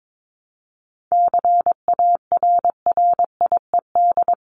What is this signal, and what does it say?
Morse code, heard as a keyed tone, about 22 wpm, 700 Hz. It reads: CARRIED
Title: ED Suffix - Top 100 Words with Base Words 22wpm